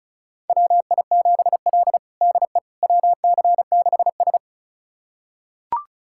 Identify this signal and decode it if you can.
Morse code, heard as a keyed tone, about 36 wpm, 700 Hz. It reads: WI7L DE WC6S
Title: USA Callsign DE Callsign - 4 Characters - 35wpm